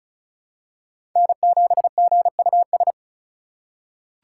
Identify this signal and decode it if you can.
Morse code, heard as a keyed tone, about 36 wpm, 700 Hz. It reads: N7GUS
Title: USA Call Signs - 4x Character Spacing - 35wpm